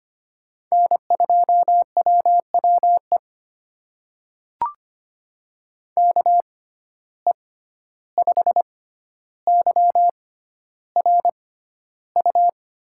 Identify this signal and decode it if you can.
Morse code, heard as a keyed tone, about 25 wpm, spaced as for 12 wpm, 700 Hz. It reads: N2WWE KE5YRU
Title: USA Call Signs - 6x Character Spacing - 25wpm